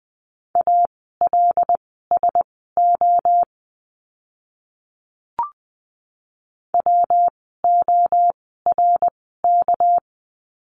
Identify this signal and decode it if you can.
Morse code, heard as a keyed tone, about 20 wpm, spaced as for 15 wpm, 700 Hz. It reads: ALSO WORK
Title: 244 Lesson 39f - Letter z - Words 20wpm